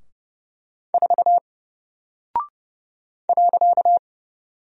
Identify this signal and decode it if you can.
Morse code, heard as a keyed tone, about 32 wpm, 700 Hz. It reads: 4 .
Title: Single Character (Letter:Number:Punctuation:Prosign) Alt 30wpm